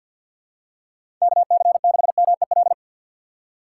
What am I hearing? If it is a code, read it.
Morse code, heard as a keyed tone, about 48 wpm, 700 Hz. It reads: KC6MEL
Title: USA Call Signs - 4x Character Spacing - 50wpm